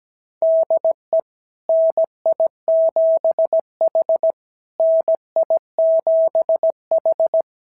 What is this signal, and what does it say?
Morse code, heard as a keyed tone, about 17 wpm, 660 Hz. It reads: DE NI7H NI7H
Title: SOTA QSO Practice 17wpm